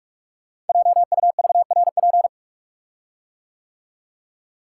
Morse code, heard as a keyed tone, about 44 wpm, 700 Hz. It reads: JUVRP
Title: Code Groups - 5 Letters - 45wpm